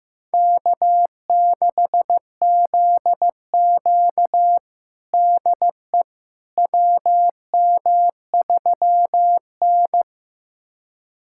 Morse code, heard as a keyed tone, about 15 wpm, 700 Hz. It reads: K6ZQ DE WM3N
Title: USA Callsign DE Callsign - 4 Characters - 15wpm